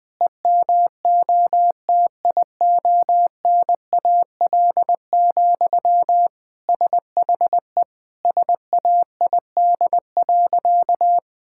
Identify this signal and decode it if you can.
Morse code, heard as a keyed tone, about 21 wpm, 700 Hz. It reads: EMOTIONAL, SHE SAID.
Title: Sentences from Top 1000 Words - Review Part 5 - 20wpm